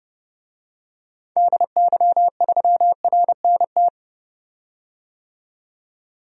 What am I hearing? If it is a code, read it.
Morse code, heard as a keyed tone, about 30 wpm, 700 Hz. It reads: DY3RNT
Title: DXCC-100 Call Signs - 6 Characters - 30wpm